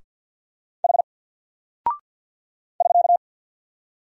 Morse code, heard as a keyed tone, about 50 wpm, 700 Hz. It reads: H <SK>
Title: Single Character (Letter:Number:Punctuation:Prosign) 50wpm